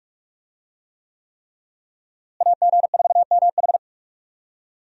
Morse code, heard as a keyed tone, about 43 wpm, 700 Hz. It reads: AG4MH
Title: USA Call Signs - 5 Characters - Repeated 3x - 45wpm